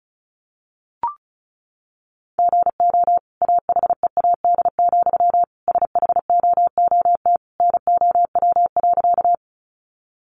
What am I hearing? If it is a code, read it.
Morse code, heard as a keyed tone, about 34 wpm, 700 Hz. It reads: GO AHEAD, SHOOT NOW.
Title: Sentences from Top 1200 Words 35wpm